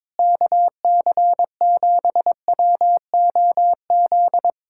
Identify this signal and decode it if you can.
Morse code, heard as a keyed tone, about 22 wpm, 700 Hz. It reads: KC7WOZ
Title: Speed-Racing Common QSO Phrases Encore - 22wpm